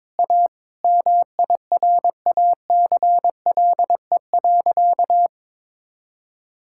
Morse code, heard as a keyed tone, about 22 wpm, 700 Hz. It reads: A MIRACLE.
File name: Sentences with 7-Word Phrases - 22wpm